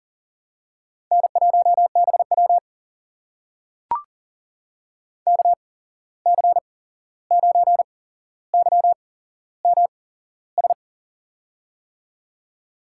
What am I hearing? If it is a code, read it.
Morse code, heard as a keyed tone, about 40 wpm, spaced as for 15 wpm, 690 Hz. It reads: N1BW KC9YMS
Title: USA Call Signs - 8x Character Spacing - 40wpm